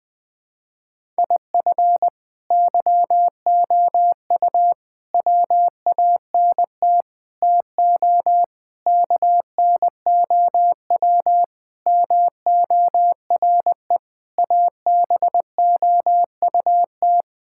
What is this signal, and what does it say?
Morse code, heard as a keyed tone, about 20 wpm, 700 Hz. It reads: IF YOU WANT TO KNOW MORE ABOUT
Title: Sets of 8 Words - 20wpm